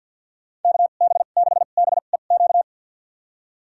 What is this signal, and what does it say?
Morse code, heard as a keyed tone, about 49 wpm, spaced as for 36 wpm, 700 Hz. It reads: KB6BEQ
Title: USA Call Signs - 2x Character Spacing - 50wpm